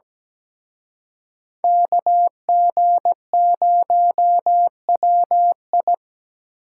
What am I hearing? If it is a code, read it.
Morse code, heard as a keyed tone, about 20 wpm, 700 Hz. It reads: KG0WI
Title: Speed Racing - USA Call Signs 17wpm